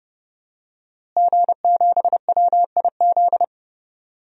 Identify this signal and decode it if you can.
Morse code, heard as a keyed tone, about 30 wpm, 700 Hz. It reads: G7WIZ
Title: DXCC-100 Call Signs - 5 Characters - Repeated 3x - 30wpm